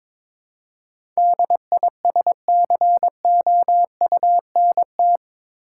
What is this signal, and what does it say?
Morse code, heard as a keyed tone, about 23 wpm, 700 Hz. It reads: DISCOUNT
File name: practice / 22wpm / Eight Letter Words - 22wpm